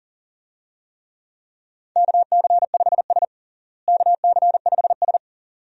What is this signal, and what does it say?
Morse code, heard as a keyed tone, about 40 wpm, 700 Hz. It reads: KC5S KC5S
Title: USA Call Signs - 4 Characters - Repeated 2x - 40wpm